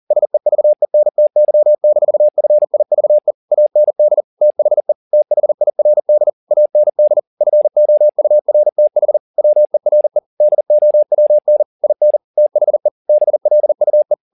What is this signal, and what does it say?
Morse code, heard as a keyed tone, about 40 wpm, 600 Hz. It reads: SEVENTY-FIVE AND THE THIRD AND FOURTH WERE DOWN IN THE BLUE